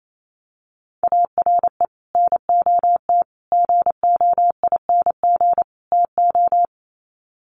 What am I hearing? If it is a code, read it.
Morse code, heard as a keyed tone, about 28 wpm, 700 Hz. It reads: ARE NOT GOING TO
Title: Sets of 4 Words 28wpm